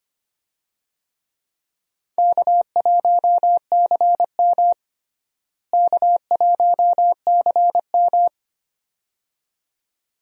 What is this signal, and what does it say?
Morse code, heard as a keyed tone, about 25 wpm, 700 Hz. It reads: K1CM K1CM
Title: USA Call Signs - 4 Characters - Repeated 2x - 25wpm